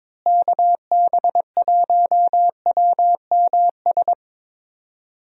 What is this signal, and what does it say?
Morse code, heard as a keyed tone, about 22 wpm, 700 Hz. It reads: KB1WMS
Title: USA Call Signs - 6 Characters - Repeated 2x - 22wpm